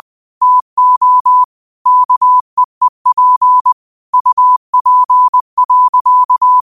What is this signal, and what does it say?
Morse code, heard as a keyed tone, about 20 wpm, 1000 Hz. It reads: TO KEEP UP.
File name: 20wpm